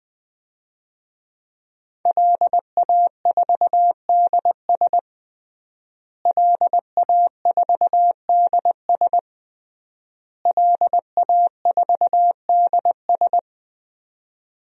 Morse code, heard as a keyed tone, about 20 wpm, 700 Hz. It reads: LA4DS LA4DS LA4DS
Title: DXCC-100 Call Signs - 5 Characters - Repeated 3x - 20wpm